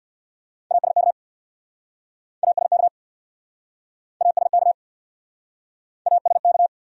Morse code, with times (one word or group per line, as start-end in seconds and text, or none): 0.7-1.1 ASK
2.4-2.9 ASK
4.2-4.7 ASK
6.1-6.7 ASK